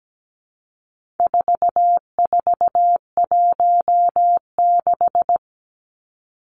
Morse code, heard as a keyed tone, about 17 wpm, 700 Hz. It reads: 4416
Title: Numbers - 4-Digit 17wpm